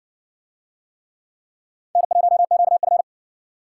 Morse code, heard as a keyed tone, about 60 wpm, 700 Hz. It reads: N1QF